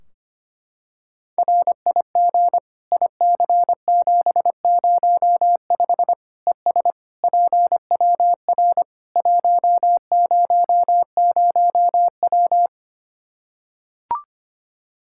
Morse code, heard as a keyed tone, about 25 wpm, 700 Hz. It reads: RIG IC705 ES PWR 100W
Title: Practice QSO - Exchange 2 - Info - 25wpm